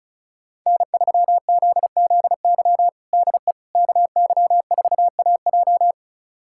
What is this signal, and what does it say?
Morse code, heard as a keyed tone, about 36 wpm, 700 Hz. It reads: N3ZZY DE KY4AJ
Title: USA Callsign DE Callsign - 5 Characters - 35wpm